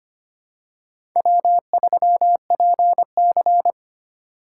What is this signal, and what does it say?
Morse code, heard as a keyed tone, about 25 wpm, 700 Hz. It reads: W3PC
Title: USA Call Signs - 4 Characters - 2x Character Spacing - 25wpm